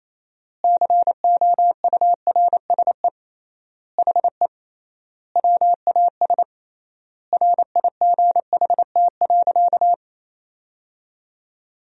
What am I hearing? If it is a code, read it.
Morse code, heard as a keyed tone, about 28 wpm, 700 Hz. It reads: COURSE HE WAS RIGHT.
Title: 3x Word-Spacing - Sentences from Top 500 Words - 28wpm